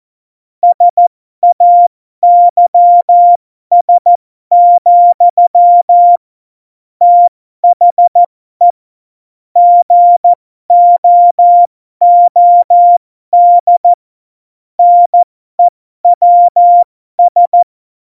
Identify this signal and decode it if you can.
Morse code, heard as a keyed tone, about 14 wpm, spaced as for 12 wpm, 700 Hz. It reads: SAYS, THE GOOD NEWS
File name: Morsekurs / Witze